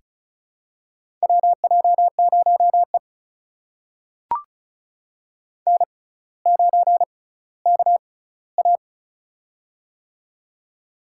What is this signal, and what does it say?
Morse code, heard as a keyed tone, about 36 wpm, spaced as for 18 wpm, 700 Hz. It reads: WJ0E N9KA
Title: USA Call Signs - 4 Characters - 6x Character Spacing - 35wpm